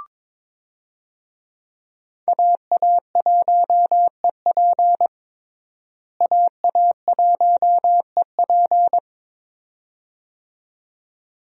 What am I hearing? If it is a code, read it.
Morse code, heard as a keyed tone, about 21 wpm, 700 Hz. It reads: AA1EP AA1EP
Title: USA Call Signs - 5 Characters - Repeated 2x - 22wpm